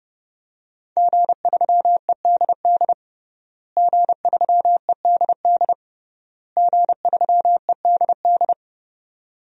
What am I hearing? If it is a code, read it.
Morse code, heard as a keyed tone, about 28 wpm, 700 Hz. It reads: G3EDD G3EDD G3EDD
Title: DXCC-100 Call Signs - 5 Characters - Repeated 3x - 30wpm